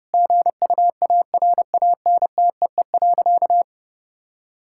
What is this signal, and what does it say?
Morse code, heard as a keyed tone, about 30 wpm, 700 Hz. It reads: GUARANTEE.
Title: Sentences from Top 1600 Words 30wpm